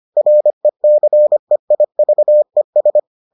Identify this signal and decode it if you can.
Morse code, heard as a keyed tone, about 25 wpm, 600 Hz. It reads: RECEIVES